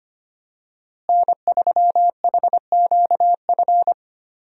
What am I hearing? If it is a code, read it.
Morse code, heard as a keyed tone, about 25 wpm, 700 Hz. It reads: N3HQF
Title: USA Call Signs - 5 Characters - Repeated 2x - 25wpm